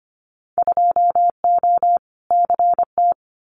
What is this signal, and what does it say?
Morse code, heard as a keyed tone, about 25 wpm, 700 Hz. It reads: 2O CT
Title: Winter Field Day Exchanges 25wpm